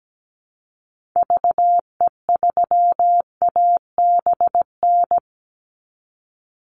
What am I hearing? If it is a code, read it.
Morse code, heard as a keyed tone, about 17 wpm, 700 Hz. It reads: VE3ABN
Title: Call Signs Repeated 2x - 17wpm